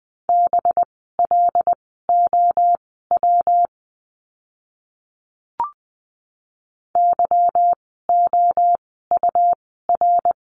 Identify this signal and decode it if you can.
Morse code, heard as a keyed tone, about 20 wpm, spaced as for 14 wpm, 700 Hz. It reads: BLOW YOUR
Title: 185 Lesson 30f - Letter b - Words 20wpm